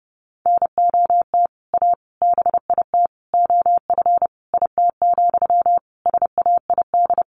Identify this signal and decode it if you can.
Morse code, heard as a keyed tone, about 30 wpm, 700 Hz. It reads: NOT A BIT OF IT, SAID
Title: Sentences from Top 1400 Words 30wpm